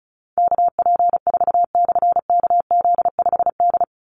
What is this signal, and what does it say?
Morse code, heard as a keyed tone, about 34 wpm, 700 Hz. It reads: KP4/KZ5D
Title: Call Signs - 35wpm